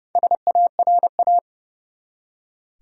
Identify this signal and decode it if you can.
Morse code, heard as a keyed tone, about 30 wpm, 700 Hz. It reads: SARA
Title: USA Common Names - Max 5 Letters - 30wpm